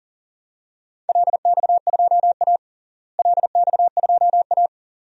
Morse code, heard as a keyed tone, about 40 wpm, 700 Hz. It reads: LX2A LX2A
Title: DXCC-100 Call Signs - 4 Characters - Repeated 2x - 40wpm